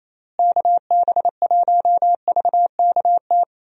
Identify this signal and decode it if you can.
Morse code, heard as a keyed tone, about 28 wpm, 700 Hz. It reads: KB1VKT
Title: USA Call Signs - 6 Characters - Repeated 3x - 28wpm